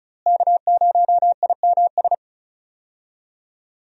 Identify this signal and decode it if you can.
Morse code, heard as a keyed tone, about 36 wpm, 700 Hz. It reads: K0IMS
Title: USA Call Signs - 5 Characters - Repeated 2x - 35wpm